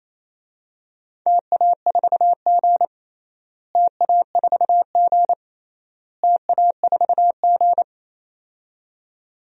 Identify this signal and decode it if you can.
Morse code, heard as a keyed tone, about 28 wpm, 700 Hz. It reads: TA4G TA4G TA4G
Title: DXCC-100 Call Signs - 4 Characters - Repeated 3x - 28wpm